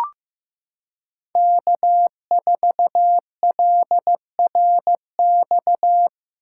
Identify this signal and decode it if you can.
Morse code, heard as a keyed tone, about 15 wpm, 700 Hz. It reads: K4LRX
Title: K1USN SST QSO Practice - 15wpm